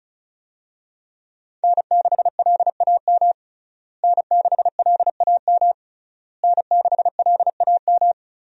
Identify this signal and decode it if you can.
Morse code, heard as a keyed tone, about 36 wpm, 700 Hz. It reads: N6LAM N6LAM N6LAM
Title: USA Call Signs - 5 Characters - Repeated 3x - 35wpm